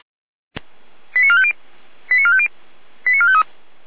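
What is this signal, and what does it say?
Five-tone selective call (ZVEI), heard as quick runs of five tones, tones of about 80 ms. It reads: 89450 89450 89453
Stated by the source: ZVEI five-tone selective call received on SDR.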